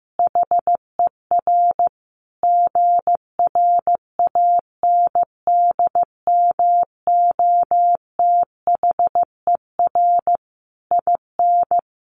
Morse code, heard as a keyed tone, about 14 wpm, 700 Hz. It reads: HER GRANDMOTHER IN
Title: Sentences from Top 1200 Words 15wpm